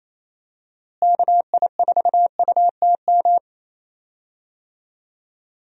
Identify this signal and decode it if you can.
Morse code, heard as a keyed tone, about 27 wpm, 700 Hz. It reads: KI4UTM
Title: USA Call Signs - 6 Characters - Repeated 3x - 28wpm